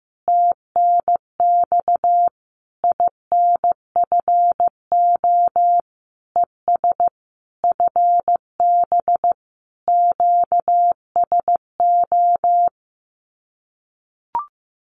Morse code, heard as a keyed tone, about 15 wpm, 700 Hz. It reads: TNX INFO ES FB QSO
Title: Common QSO Phrases Encore 15wpm